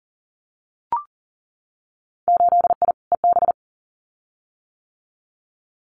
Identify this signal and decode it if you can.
Morse code, heard as a keyed tone, about 42 wpm, 700 Hz. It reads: 8I EB
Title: Winter Field Day Exchanges Encore 40wpm